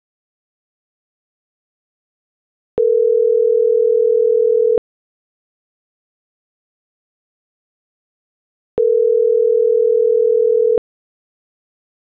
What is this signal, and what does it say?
Ringback tone (North America), 440 + 480 Hz.